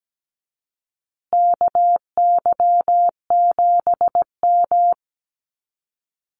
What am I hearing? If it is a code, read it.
Morse code, heard as a keyed tone, about 17 wpm, 700 Hz. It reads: KY7M
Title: Call Signs Repeated 3x - 17wpm